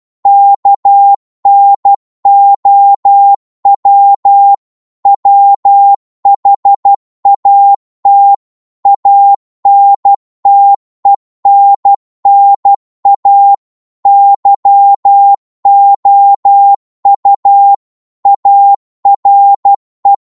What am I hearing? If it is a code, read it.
Morse code, heard as a keyed tone, about 12 wpm, 800 Hz. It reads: KNOW WHAT ANTENNA YOU ARE
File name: Morse / qso-12wpm